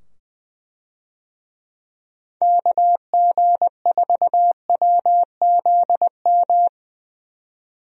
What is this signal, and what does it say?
Morse code, heard as a keyed tone, about 20 wpm, 700 Hz. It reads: KG4WZM
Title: USA Call Signs - 6 Characters - 20wpm